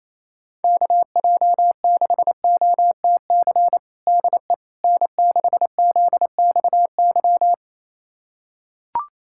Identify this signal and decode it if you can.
Morse code, heard as a keyed tone, about 27 wpm, 700 Hz. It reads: KJ6OTC DE N6ZXY
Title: USA Callsign DE Callsign 28wpm